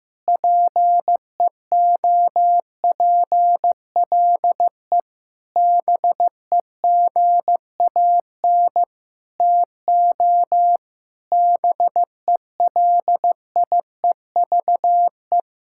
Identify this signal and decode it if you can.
Morse code, heard as a keyed tone, about 16 wpm, 700 Hz. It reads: PEOPLE BEGAN TO BELIEVE
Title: Sentences from Top 1600 Words 15wpm